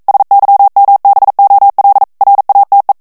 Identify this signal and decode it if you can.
Morse code, heard as a keyed tone, about 42 wpm, 800 Hz. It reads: SYMBOL RATE